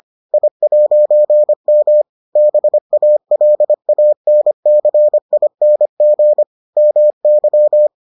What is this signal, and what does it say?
Morse code, heard as a keyed tone, about 25 wpm, 600 Hz. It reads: I'M BALANCING MY